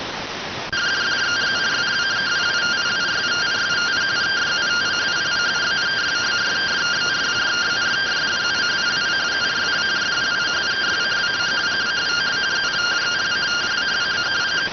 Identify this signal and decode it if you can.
Two-tone teleprinter signal, about 170 Hz shift.